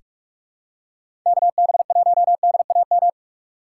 Morse code, heard as a keyed tone, about 43 wpm, 700 Hz. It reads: KB1DAM
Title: USA Call Signs - 6 Characters - 6x Character Spacing - 45wpm